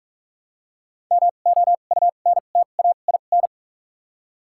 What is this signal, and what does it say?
Morse code, heard as a keyed tone, about 44 wpm, spaced as for 32 wpm, 700 Hz. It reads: MOUNTAIN
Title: 2x Character Spacing - Top 300 Words - 45wpm